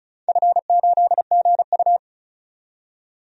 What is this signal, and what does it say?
Morse code, heard as a keyed tone, about 35 wpm, 700 Hz. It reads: F8GU
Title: DXCC-100 Call Signs - 4 Characters - 35wpm